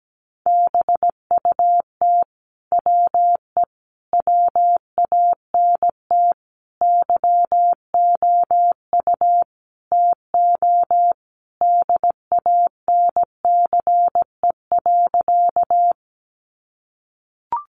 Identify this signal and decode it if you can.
Morse code, heard as a keyed tone, about 17 wpm, 700 Hz. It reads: BUT WE WANT YOU TO DANCE.
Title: Sentences from Top 700 Words 17wpm